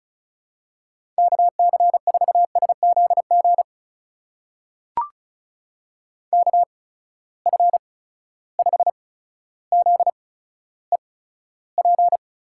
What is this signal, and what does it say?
Morse code, heard as a keyed tone, about 36 wpm, spaced as for 15 wpm, 700 Hz. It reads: KC4SZG KF5ZEP
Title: USA Call Signs - 6 Characters - 8x Character Spacing - 35wpm